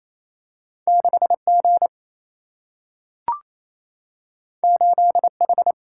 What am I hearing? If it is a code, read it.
Morse code, heard as a keyed tone, about 28 wpm, 700 Hz. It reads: 6 G 8 H
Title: Single-Letter Single-Number 28wpm